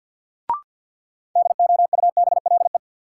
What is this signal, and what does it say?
Morse code, heard as a keyed tone, about 52 wpm, 700 Hz. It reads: DOUBLE
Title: Sentences from Top 1400 Words - New Word Drill - 50wpm